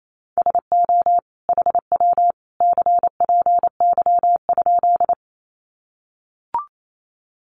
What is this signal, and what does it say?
Morse code, heard as a keyed tone, about 28 wpm, 700 Hz. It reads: SO HW CPY?
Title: Common QSO Phrases 28wpm